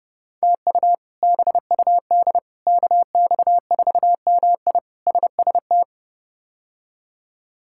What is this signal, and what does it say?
Morse code, heard as a keyed tone, about 30 wpm, 700 Hz. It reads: TU BUD KX4MI SST
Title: K1USN SST QSO Practice - 30wpm